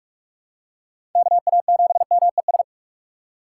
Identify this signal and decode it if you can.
Morse code, heard as a keyed tone, about 43 wpm, 700 Hz. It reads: KA7MES
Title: USA Call Signs - 6 Characters - 3x Character Spacing - 45wpm